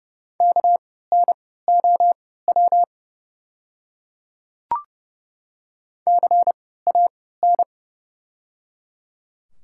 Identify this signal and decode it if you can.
Morse code, heard as a keyed tone, about 30 wpm, spaced as for 17 wpm, 700 Hz. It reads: KNOW CAN
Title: 3x Character Spacing - Top 100 Words - 30wpm